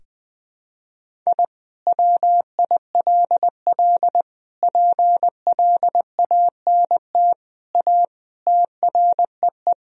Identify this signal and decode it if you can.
Morse code, heard as a keyed tone, about 20 wpm, 700 Hz. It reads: I WILL PLANT A TREE
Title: Sentences from Top 1000 Words - Review Part 1 - 20wpm